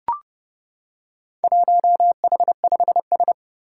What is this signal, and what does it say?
Morse code, heard as a keyed tone, about 30 wpm, 700 Hz. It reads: 1H5S
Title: Four Challenging Letter-Number Combinations 30wpm